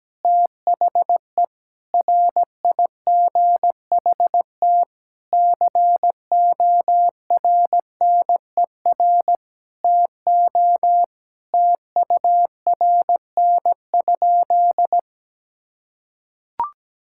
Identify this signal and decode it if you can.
Morse code, heard as a keyed tone, about 17 wpm, 700 Hz. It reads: THE RIGHT CORNER TO TURN?
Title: Sentences from Top 1600 Words 17wpm